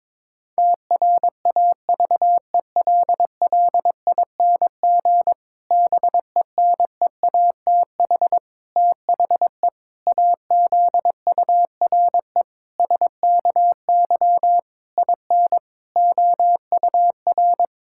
Morse code, heard as a keyed tone, about 21 wpm, 700 Hz. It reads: TRAVELLING BENEATH THE AZURE SKY IN OUR
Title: Pangram Challenge v2 - 22wpm